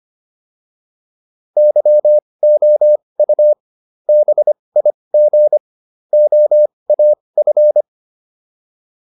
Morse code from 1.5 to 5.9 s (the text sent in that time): YOU BIG